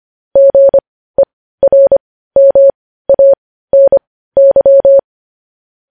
Morse code, heard as a keyed tone, about 25 wpm, spaced as for 15 wpm, 550 Hz. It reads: GERMANY
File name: podcast / News headlines-1x-25x15